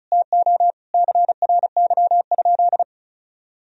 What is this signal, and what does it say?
Morse code, heard as a keyed tone, about 36 wpm, 700 Hz. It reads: TO CRY?
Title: Sentences from Top 1000 Words - Review Part 2 - 35wpm